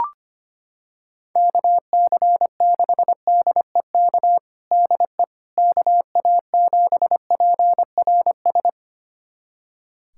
Morse code, heard as a keyed tone, about 25 wpm, 700 Hz. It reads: KC6DEK DE KA7PRS
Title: USA Callsign DE Callsign - 6 Characters - 25wpm